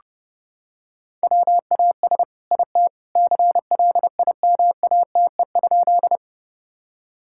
Morse code, heard as a keyed tone, about 30 wpm, 700 Hz. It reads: WAS IT CLIMATE?
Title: Sentences from Top 1600 Words 30wpm